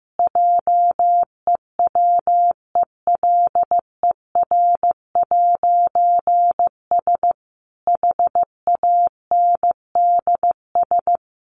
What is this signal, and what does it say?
Morse code, heard as a keyed tone, about 15 wpm, 700 Hz. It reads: JEWELER'S HANDS
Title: A Sound Of Thunder 15wpm